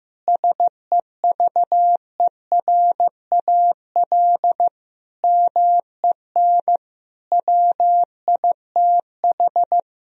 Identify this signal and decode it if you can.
Morse code, heard as a keyed tone, about 16 wpm, 700 Hz. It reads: SEVERAL MEN WITH
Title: Sentences from Top 1000 Words - Review Part 2 - 15wpm